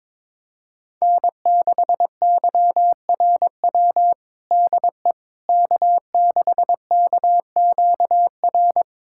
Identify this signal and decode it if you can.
Morse code, heard as a keyed tone, about 22 wpm, 700 Hz. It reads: N6YRW DE K6KQR
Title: USA Callsign DE Callsign - 5 Characters - 22wpm